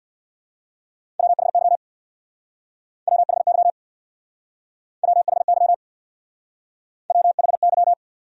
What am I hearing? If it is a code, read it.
Morse code, heard as a keyed tone, about 61 wpm, 700 Hz. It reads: WHY WHY WHY WHY
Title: Speed-Racing Top 200 Words 50wpm